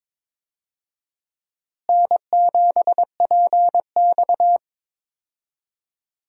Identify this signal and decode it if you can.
Morse code, heard as a keyed tone, about 22 wpm, 700 Hz. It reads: N7PX
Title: USA Call Signs - 4 Characters - Repeated 2x - 22wpm